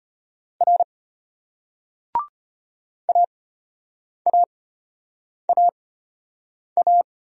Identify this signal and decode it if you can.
Morse code, heard as a keyed tone, about 33 wpm, spaced as for 8 wpm, 700 Hz. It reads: R AAAA